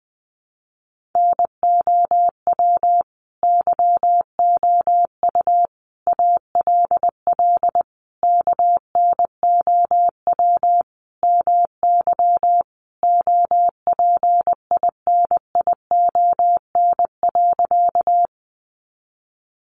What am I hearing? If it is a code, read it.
Morse code, heard as a keyed tone, about 20 wpm, 700 Hz. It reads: NOW YOU ALL KNOW MY OPINION.
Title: Sentences from Top 1200 Words 20wpm